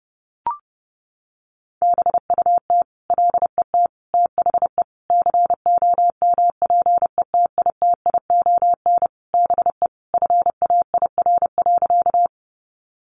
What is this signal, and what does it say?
Morse code, heard as a keyed tone, about 30 wpm, 700 Hz. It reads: BUT LET THE COMPETITION BE FAIR.
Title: Sentences from Top 1300 Words 30wpm